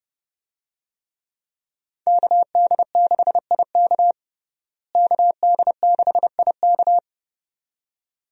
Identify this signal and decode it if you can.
Morse code, heard as a keyed tone, about 31 wpm, 700 Hz. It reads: KD6IK KD6IK
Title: USA Call Signs - 5 Characters - Repeated 2x - 30wpm